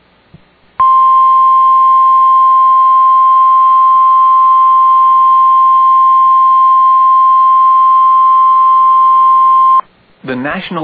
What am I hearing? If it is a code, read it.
Emergency Alert System (SAME) tones (the 1050 Hz attention tone), with speech. It is heard from 0.8 to 9.9 s.